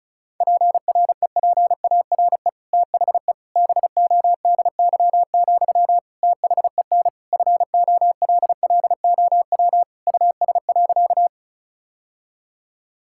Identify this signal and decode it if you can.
Morse code, heard as a keyed tone, about 36 wpm, 700 Hz. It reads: PREPARE THE BODY, THEN FOLLOW US.
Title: Sentences from Top 1000 Words - Review Part 6 - 35wpm